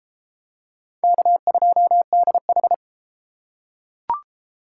Morse code, heard as a keyed tone, about 31 wpm, 700 Hz. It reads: K2DH